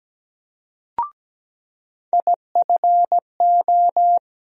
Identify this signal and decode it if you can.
Morse code, heard as a keyed tone, about 17 wpm, 700 Hz. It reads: IFO
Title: USA Call Sign Postfixes - 3 Letters 17wpm